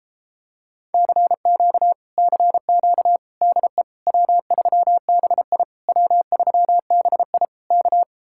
Morse code, heard as a keyed tone, about 33 wpm, 700 Hz. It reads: CQ CQ DE W3BI W3BI K